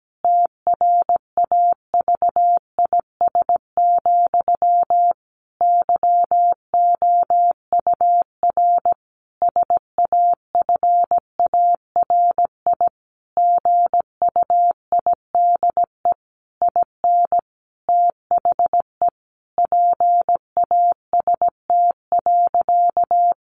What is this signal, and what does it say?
Morse code, heard as a keyed tone, about 17 wpm, 700 Hz. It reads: TRAVIS, YOUR SAFARI GUIDE IN THE PAST.
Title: A Sound Of Thunder 17wpm